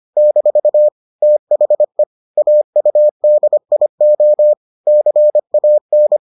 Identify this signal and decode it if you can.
Morse code, heard as a keyed tone, about 25 wpm, 600 Hz. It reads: - THE AUDIO CAN